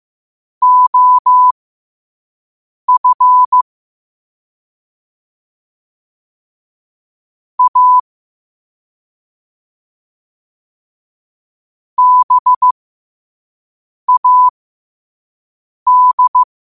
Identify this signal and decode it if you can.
Morse code, heard as a keyed tone, about 15 wpm, spaced as for 5 wpm, 1000 Hz. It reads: OF A BAD
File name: qotdfiles / 5wpm